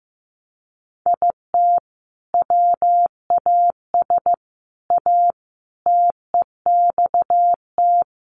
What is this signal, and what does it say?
Morse code, heard as a keyed tone, about 15 wpm, 700 Hz. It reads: IT WAS A TEXT
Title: Sentences from Top 1200 Words 15wpm